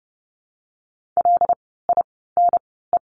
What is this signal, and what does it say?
Morse code, heard as a keyed tone, about 30 wpm, spaced as for 17 wpm, 700 Hz. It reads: LINE
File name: course / 30:10wpm / 115 Lesson 19f - Letter w - Words 30wpm